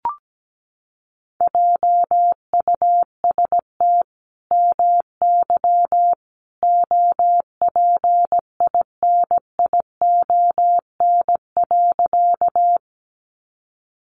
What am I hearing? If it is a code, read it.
Morse code, heard as a keyed tone, about 17 wpm, 700 Hz. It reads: JUST MY OPINION.
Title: Sentences from Top 1200 Words 17wpm